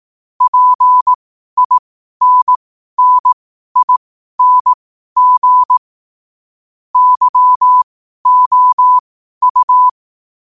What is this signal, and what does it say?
Morse code, heard as a keyed tone, about 18 wpm, spaced as for 12 wpm, 1000 Hz. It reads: PINNING YOU
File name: qotdfiles / 12wpm